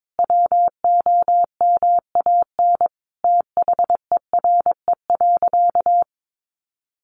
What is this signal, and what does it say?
Morse code, heard as a keyed tone, about 22 wpm, 700 Hz. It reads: WOMAN THERE.
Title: Sentences from Top 1100 Words 22wpm